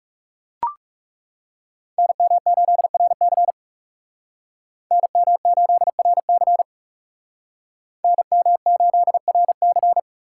Morse code, heard as a keyed tone, about 39 wpm, 700 Hz. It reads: NM8RC NM8RC NM8RC